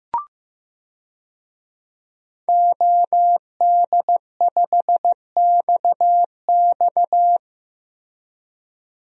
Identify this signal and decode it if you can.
Morse code, heard as a keyed tone, about 15 wpm, 700 Hz. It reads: OD5XX
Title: DXCC-100 Call Signs - 5 Characters - Repeated 2x - 15wpm